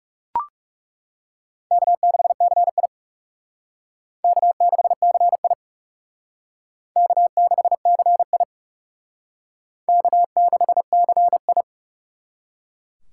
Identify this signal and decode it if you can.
Morse code, heard as a keyed tone, about 37 wpm, 700 Hz. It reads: K6CI K6CI K6CI K6CI
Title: Speed Racing - USA 1x2 Call Signs 30wpm